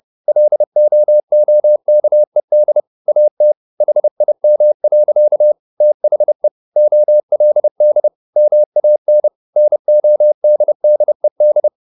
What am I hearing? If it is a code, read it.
Morse code, heard as a keyed tone, about 30 wpm, 600 Hz. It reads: LOOKED AT HIM. THE OLD MAN NODDED
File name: the old man and the sea / the old man and the sea 1 30wpm